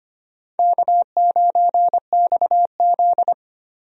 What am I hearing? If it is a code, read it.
Morse code, heard as a keyed tone, about 25 wpm, 700 Hz. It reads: K9XZ
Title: USA Call Signs - 4 Characters - Repeated 3x - 25wpm